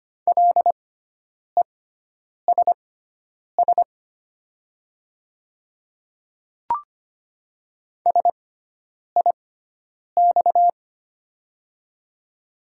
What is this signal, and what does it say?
Morse code, heard as a keyed tone, about 25 wpm, spaced as for 9 wpm, 700 Hz. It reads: LESS SIX
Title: 6x Character Spacing - Top 500 Words - 25wpm